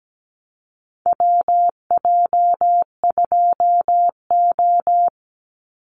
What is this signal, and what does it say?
Morse code, heard as a keyed tone, about 17 wpm, 700 Hz. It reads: WJ2O
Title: Call Signs Repeated 3x - 17wpm